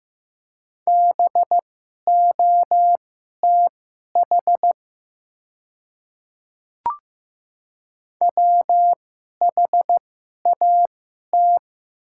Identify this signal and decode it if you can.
Morse code, heard as a keyed tone, about 15 wpm, spaced as for 11 wpm, 700 Hz. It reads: BOTH WHAT
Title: 2x Character Spacing - Top 500 Words - 15wpm